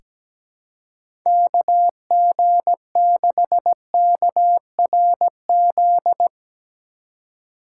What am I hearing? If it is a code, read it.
Morse code, heard as a keyed tone, about 17 wpm, 700 Hz. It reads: KG6KRZ
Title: USA Call Signs - 6 Characters - 8x Character Spacing - 17wpm